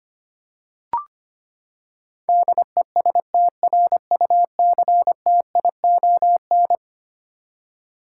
Morse code, heard as a keyed tone, about 25 wpm, 700 Hz. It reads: DESTRUCTION
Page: morse-code-ninja-practice-25wpm